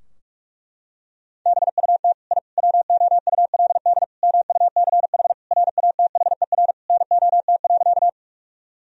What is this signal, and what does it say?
Morse code, heard as a keyed tone, about 46 wpm, 700 Hz. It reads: BUT I WOULD MUCH RATHER NOT.
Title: Sentences from Top 1000 Words - Review Part 4 - 45wpm